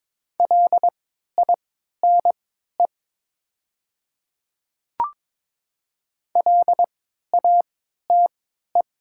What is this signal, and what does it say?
Morse code, heard as a keyed tone, about 21 wpm, spaced as for 12 wpm, 700 Hz. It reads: LINE LATE
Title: 3x Character Spacing - Top 500 Words - 22wpm